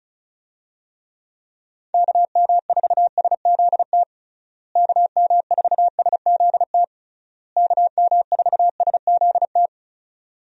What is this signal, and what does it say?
Morse code, heard as a keyed tone, about 34 wpm, 700 Hz. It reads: KM4SZT KM4SZT KM4SZT
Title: USA Call Signs - 6 Characters - Repeated 3x - 35wpm